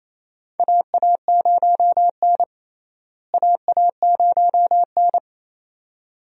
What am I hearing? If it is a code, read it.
Morse code, heard as a keyed tone, about 28 wpm, 700 Hz. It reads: AA0N AA0N
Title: USA Call Signs - 4 Characters - Repeated 2x - 28wpm